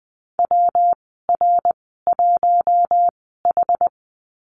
Morse code, heard as a keyed tone, about 20 wpm, spaced as for 14 wpm, 700 Hz. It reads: WR1H
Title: 193 Lesson 31g - Letter v - Call Signs 20wpm